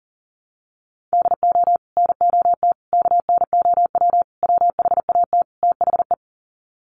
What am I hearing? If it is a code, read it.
Morse code, heard as a keyed tone, about 40 wpm, 700 Hz. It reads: DO NOT KNOW WHAT THE
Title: Sets of 5 Words 40wpm